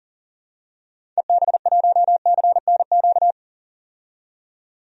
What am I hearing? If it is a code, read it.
Morse code, heard as a keyed tone, about 38 wpm, 700 Hz. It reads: EB1CNQ
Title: DXCC-100 Call Signs - 6 Characters - 40wpm